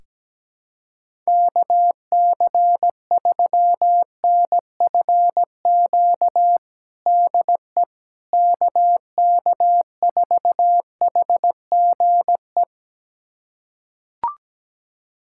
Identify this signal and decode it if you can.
Morse code, heard as a keyed tone, about 17 wpm, 700 Hz. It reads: KC3NFQ DE KK4HGE
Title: USA Callsign DE Callsign - 6 Characters - 17wpm